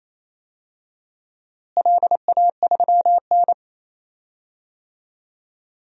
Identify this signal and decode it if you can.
Morse code, heard as a keyed tone, about 28 wpm, 700 Hz. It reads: LA3N
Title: DXCC-100 Call Signs - 4 Characters - Repeated 2x - 28wpm